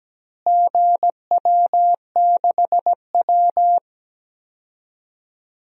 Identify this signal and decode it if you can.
Morse code, heard as a keyed tone, about 17 wpm, 700 Hz. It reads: GW6W
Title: DXCC-100 Call Signs - 4 Characters - Repeated 2x - 17wpm